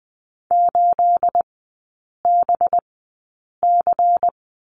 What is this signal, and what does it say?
Morse code, heard as a keyed tone, about 19 wpm, spaced as for 8 wpm, 700 Hz. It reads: 8BC
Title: 212 Lesson 34e - Number 8 - Three Characters 20wpm